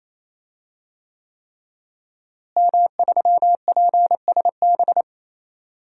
Morse code, heard as a keyed tone, about 28 wpm, 700 Hz. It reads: M3PSB
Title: DXCC-100 Call Signs - 5 Characters - Repeated 3x - 28wpm